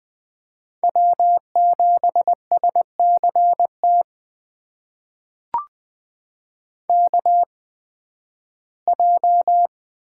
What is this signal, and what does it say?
Morse code, heard as a keyed tone, about 20 wpm, spaced as for 13 wpm, 700 Hz. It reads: W7SCT KJ
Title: USA Call Signs - 8x Character Spacing - 20wpm